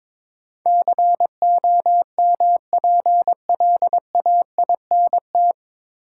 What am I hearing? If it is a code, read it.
Morse code, heard as a keyed tone, about 22 wpm, 700 Hz. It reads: COMPLAINT
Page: Morse Code Ninja practice 22wpm